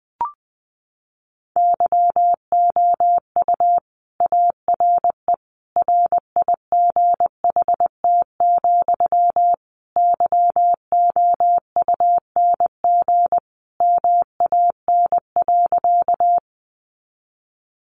Morse code, heard as a keyed tone, about 20 wpm, 700 Hz. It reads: YOU ARE RIGHT, YOUNG MAN.
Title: Sentences from Top 400 Words 20wpm